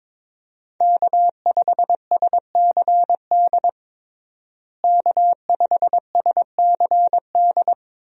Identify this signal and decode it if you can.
Morse code, heard as a keyed tone, about 22 wpm, 700 Hz. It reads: K5SCD K5SCD
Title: USA Call Signs - 5 Characters - Repeated 2x - 22wpm